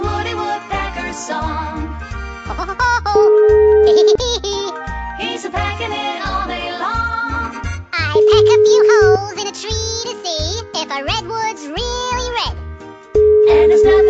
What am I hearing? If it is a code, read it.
Ringback tone (Europe, 425 Hz), with speech.